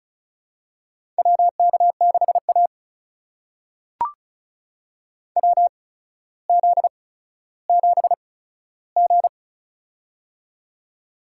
Morse code, heard as a keyed tone, about 36 wpm, spaced as for 15 wpm, 700 Hz. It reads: WK6A WZ7G
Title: USA Call Signs - 4 Characters - 8x Character Spacing - 35wpm